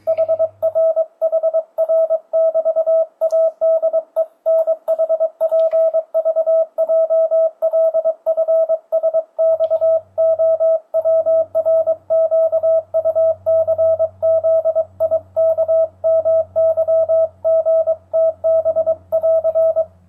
Morse code, heard as a keyed tone, about 22 wpm, 650 Hz, with radio noise. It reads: HRHR<BT>ADENHPVJLFSXOWRQUCZIKMYGTB<AR>
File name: 2025cw / speedup / 23wpm